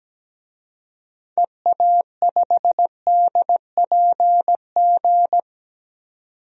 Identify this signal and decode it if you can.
Morse code, heard as a keyed tone, about 17 wpm, 700 Hz. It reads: EA5DPG
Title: DXCC-100 Call Signs - 6 Characters - 17wpm